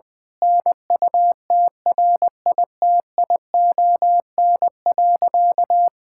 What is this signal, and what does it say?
Morse code, heard as a keyed tone, about 21 wpm, 700 Hz. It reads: NUTRITION.
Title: Difficult Sentences with Hints - 20wpm